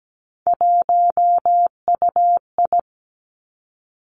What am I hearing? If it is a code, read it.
Morse code, heard as a keyed tone, about 17 wpm, 700 Hz. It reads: 1UI
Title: Three Challenging Letter-Number Combinations 17wpm